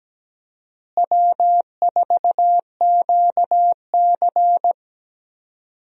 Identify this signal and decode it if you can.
Morse code, heard as a keyed tone, about 17 wpm, 700 Hz. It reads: W4QC
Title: USA Call Signs - 4 Characters - Repeated 2x - 17wpm